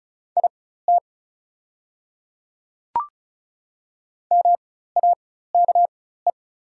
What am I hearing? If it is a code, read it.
Morse code, heard as a keyed tone, about 34 wpm, spaced as for 16 wpm, 700 Hz. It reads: IT MAKE
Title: 4x Character Spacing - Top 100 Words - 35wpm